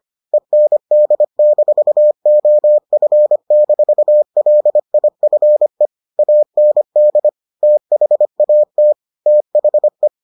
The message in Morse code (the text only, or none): END-OF-LIFE AND THAT THE